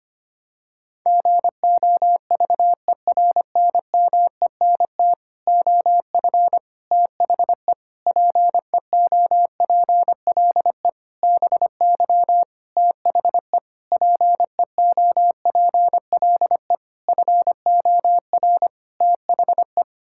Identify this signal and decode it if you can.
Morse code, heard as a keyed tone, about 25 wpm, 700 Hz. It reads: GOVERNMENT OF THE PEOPLE BY THE PEOPLE FOR THE
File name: Sets of 9 Words - 25wpm